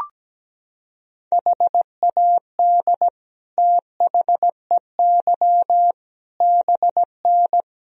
Morse code, heard as a keyed tone, about 17 wpm, 700 Hz. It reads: HAD THEY BN
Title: Sentences from Top 400 Words and Abbreviations 17wpm